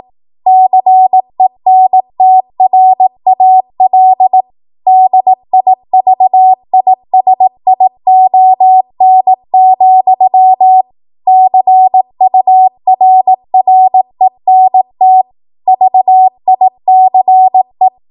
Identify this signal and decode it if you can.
Morse code, heard as a keyed tone, about 18 wpm, 750 Hz. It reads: CENTRAL DIVISION, CURRENT VICE